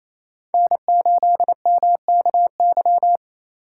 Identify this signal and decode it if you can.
Morse code, heard as a keyed tone, about 34 wpm, 700 Hz. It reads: N8MKY
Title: Speed Racing - USA 1x3 Call Signs 28wpm